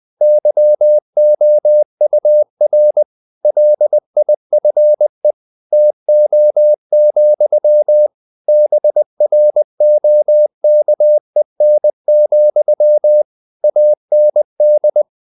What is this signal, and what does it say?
Morse code, heard as a keyed tone, about 20 wpm, 600 Hz. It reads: YOUR LIFE TO, BROKEN, AND
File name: CW / Practice / If20